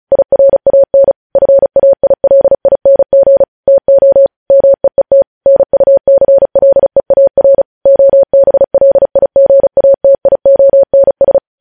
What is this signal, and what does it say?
Morse code, heard as a keyed tone, about 36 wpm, 550 Hz. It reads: IRAN FAILING TO MEET NUCLEAR OBLIGATIONS
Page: news headlines in morse code at 35 wpm